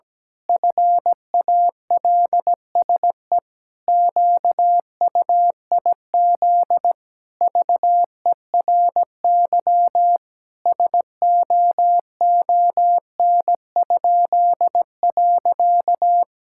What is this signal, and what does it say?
Morse code, heard as a keyed tone, about 17 wpm, 700 Hz. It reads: FALSE QUIZ VERY SOON?.
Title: Pangram Challenge v2 - 17wpm